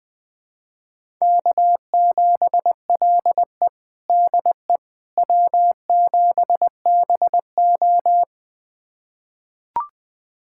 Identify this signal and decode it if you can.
Morse code, heard as a keyed tone, about 20 wpm, 700 Hz. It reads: K7LE DE W7BO
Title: USA Callsign DE Callsign - 4 Characters - 20wpm